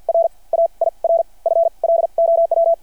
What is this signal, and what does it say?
Morse code, heard as a keyed tone, about 48 wpm, spaced as for 35 wpm, 660 Hz. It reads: R AIRFLOW
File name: archived / CRSnet